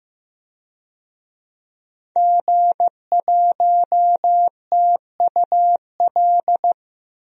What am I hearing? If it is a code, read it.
Morse code, heard as a keyed tone, about 15 wpm, 700 Hz. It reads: G1TUL
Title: DXCC-100 Call Signs - 5 Characters - Repeated 3x - 15wpm